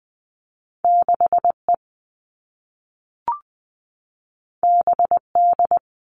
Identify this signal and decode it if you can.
Morse code, heard as a keyed tone, about 20 wpm, 700 Hz. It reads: 6E BD
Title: All Cut Numbers - 2-Digit 20wpm